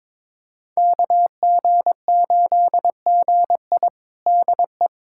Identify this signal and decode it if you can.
Morse code, heard as a keyed tone, about 22 wpm, 700 Hz. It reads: KG8GI DE
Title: USA Callsign DE Callsign 22wpm